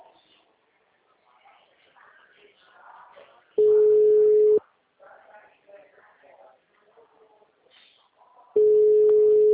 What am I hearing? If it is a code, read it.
Ringback tone (Europe, 425 Hz), with speech.
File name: out- -2015- - - .36